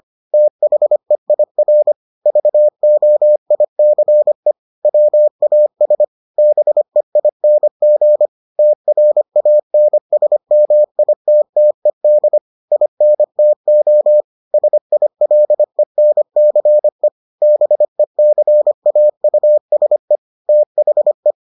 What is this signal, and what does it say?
Morse code, heard as a keyed tone, about 25 wpm, 600 Hz. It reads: THEIR VOICE WAS BEING TRANSMITTED INTO SILENCE BECAUSE THE